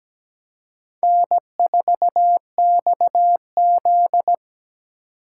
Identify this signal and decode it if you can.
Morse code, heard as a keyed tone, about 20 wpm, 700 Hz. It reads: N4XZ
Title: Speed Racing - USA Call Signs 17wpm